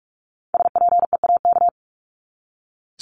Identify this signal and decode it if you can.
Morse code, heard as a keyed tone, about 44 wpm, 700 Hz. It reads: SPEAK
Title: Top 1001-1250 Words 45wpm